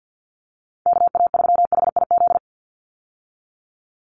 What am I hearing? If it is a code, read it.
Morse code, heard as a keyed tone, about 50 wpm, 700 Hz. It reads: KA3HIZ
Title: USA Call Signs 50wpm